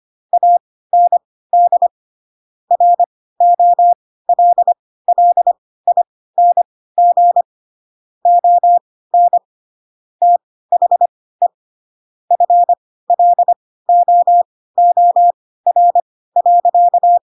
Morse code, heard as a keyed tone, about 25 wpm, spaced as for 16 wpm, 700 Hz. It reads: AND ROLLING ON THE FLOOR.
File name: Pinocchio / Pinocchio ch7 10 25 ik1hge